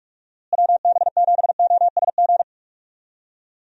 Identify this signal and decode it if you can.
Morse code, heard as a keyed tone, about 43 wpm, 700 Hz. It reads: WB7OSG
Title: USA Call Signs - 6 Characters - Repeated 3x - 45wpm